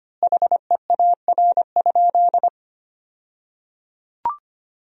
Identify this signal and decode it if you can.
Morse code, heard as a keyed tone, about 25 wpm, 700 Hz. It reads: HEAR?
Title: 2x Word-Spacing - Sentences from Top 300 Words - 25wpm